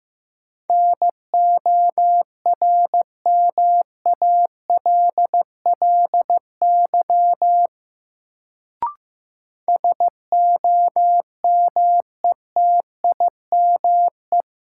Morse code, heard as a keyed tone, about 16 wpm, 700 Hz. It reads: NORMALLY SOMETIME
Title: Sentences from Top 1000 Words - Review Part 6 - New Word Drill - 15wpm